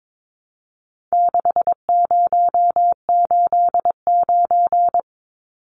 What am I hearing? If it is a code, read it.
Morse code, heard as a keyed tone, about 22 wpm, 700 Hz. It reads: 6089
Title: Numbers - 4-Digit 22wpm